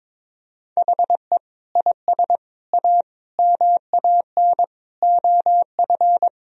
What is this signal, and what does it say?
Morse code, heard as a keyed tone, about 23 wpm, 700 Hz. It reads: HE IS A MAN OF
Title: Sentences from Top 1000 Words - Review Part 2 - 22wpm